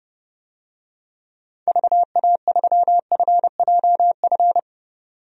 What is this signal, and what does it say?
Morse code, heard as a keyed tone, about 30 wpm, 700 Hz. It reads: VA3FJF
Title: DXCC-100 Call Signs - 6 Characters - 30wpm